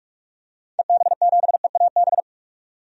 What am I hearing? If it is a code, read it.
Morse code, heard as a keyed tone, about 46 wpm, 700 Hz. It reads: EB7EAB
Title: DXCC-100 Call Signs - 6 Characters - 45wpm